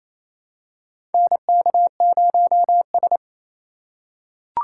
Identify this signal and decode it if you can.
Morse code, heard as a keyed tone, about 28 wpm, 700 Hz. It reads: NK0S
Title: USA Call Signs - 4 Characters - 2x Character Spacing - 28wpm